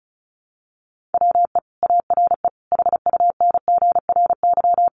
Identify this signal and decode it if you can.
Morse code, heard as a keyed tone, about 34 wpm, 700 Hz. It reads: WE ARE HUNGRY
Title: Sentences from Top 1400 Words 35wpm